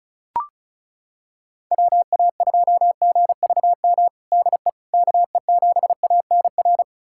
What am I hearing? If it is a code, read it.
Morse code, heard as a keyed tone, about 36 wpm, 700 Hz. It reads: WA2GVM DE KE7ANR
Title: USA Callsign DE Callsign - 6 Characters - 35wpm